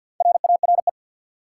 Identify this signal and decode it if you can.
Morse code, heard as a keyed tone, about 47 wpm, 700 Hz. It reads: RARE
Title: Sentences from Top 1100 Words - New Word Drill Alt - 50wpm